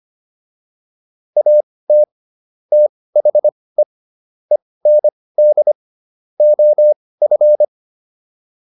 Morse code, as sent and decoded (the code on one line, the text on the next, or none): .- - / - .... / . -. -.. / --- ..-.
AT THE END OF